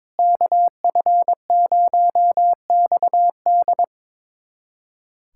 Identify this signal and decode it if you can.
Morse code, heard as a keyed tone, about 22 wpm, 700 Hz. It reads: KF0XD
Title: USA Callsign DE Callsign - 5 Characters - 22wpm